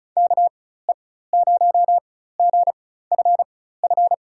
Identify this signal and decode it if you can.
Morse code, heard as a keyed tone, about 36 wpm, spaced as for 16 wpm, 700 Hz. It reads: KE0GFF
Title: USA Call Signs - 4x Character Spacing - 35wpm